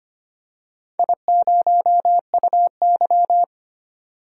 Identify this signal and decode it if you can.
Morse code, heard as a keyed tone, about 25 wpm, 700 Hz. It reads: I0UY
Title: DXCC-100 Call Signs - 4 Characters - 25wpm